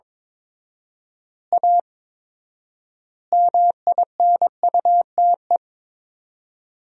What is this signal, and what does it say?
Morse code, heard as a keyed tone, about 22 wpm, 700 Hz. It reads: A MINUTE
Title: 4x Word-Spacing - Sentences from Top 500 Words - 22wpm